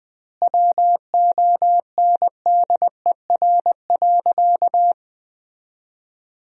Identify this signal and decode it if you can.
Morse code, heard as a keyed tone, about 20 wpm, 700 Hz. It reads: WONDER.
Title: 2x Word-Spacing - Sentences from Top 500 Words - 20wpm